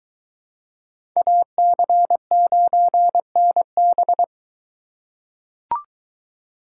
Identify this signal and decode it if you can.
Morse code, heard as a keyed tone, about 21 wpm, 700 Hz. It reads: AC9NB